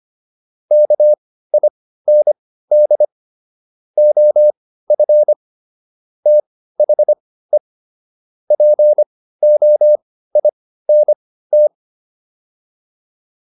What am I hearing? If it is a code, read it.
Morse code, heard as a keyed tone, about 25 wpm, spaced as for 15 wpm, 600 Hz. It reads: KIND OF THE POINT